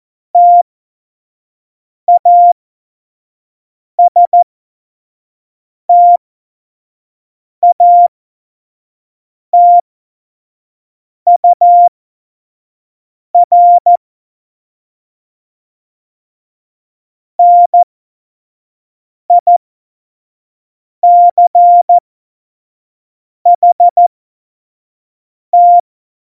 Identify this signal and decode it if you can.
Morse code, heard as a keyed tone, about 14 wpm, spaced as for 5 wpm, 700 Hz. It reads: TASTATUR NICHT
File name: Morsekurs / Witze / Witz 29. Januar 2019, 25 F70